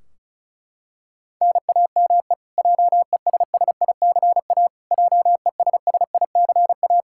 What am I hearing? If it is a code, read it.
Morse code, heard as a keyed tone, about 36 wpm, 700 Hz. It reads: NAME JESSICA JESSICA
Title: Practice QSO - Exchange 1 - Beginning - 35wpm